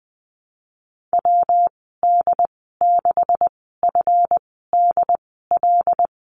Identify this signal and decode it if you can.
Morse code, heard as a keyed tone, about 20 wpm, spaced as for 14 wpm, 700 Hz. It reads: WD6FDL
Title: 147 Lesson 24g - Letter y - Call Signs 20wpm